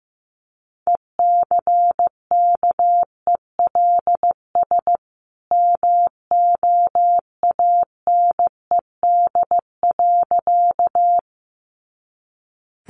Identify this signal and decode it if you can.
Morse code, heard as a keyed tone, about 15 wpm, 700 Hz. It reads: ECKELS MOANED.
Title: A Sound Of Thunder 15wpm